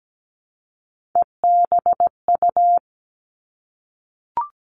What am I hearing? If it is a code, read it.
Morse code, heard as a keyed tone, about 17 wpm, 700 Hz. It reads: EBU
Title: Three Challenging Letter-Number Combinations 17wpm